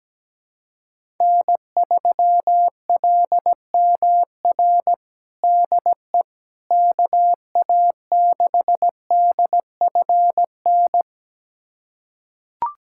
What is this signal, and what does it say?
Morse code, heard as a keyed tone, about 17 wpm, 700 Hz. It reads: N3LMR DE KA6DFN
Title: USA Callsign DE Callsign 17wpm